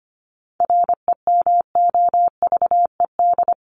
Morse code, heard as a keyed tone, about 25 wpm, 700 Hz. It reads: REMOVED
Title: ED Suffix - Top 100 Words with Base Words 25wpm